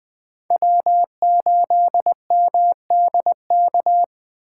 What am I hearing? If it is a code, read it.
Morse code, heard as a keyed tone, about 24 wpm, 700 Hz. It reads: W8MDK
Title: Speed Racing - USA 1x3 Call Signs 20wpm